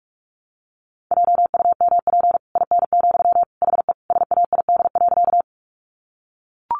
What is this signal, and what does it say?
Morse code, heard as a keyed tone, about 44 wpm, 700 Hz. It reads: JUMP IN, HE SAID.
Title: Sentences from Top 1300 Words 45wpm